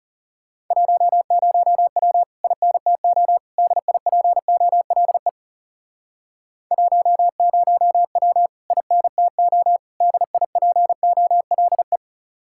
Morse code, heard as a keyed tone, about 40 wpm, 700 Hz. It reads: 10W INTO DIPOLE 10W INTO DIPOLE